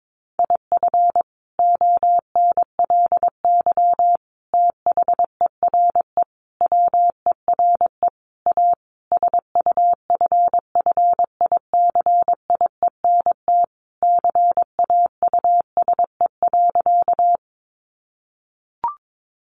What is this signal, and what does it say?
Morse code, heard as a keyed tone, about 22 wpm, 700 Hz. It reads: IF ONLY THERE WERE A SUFFICIENT CAUSE.
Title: Sentences from Top 1400 Words 22wpm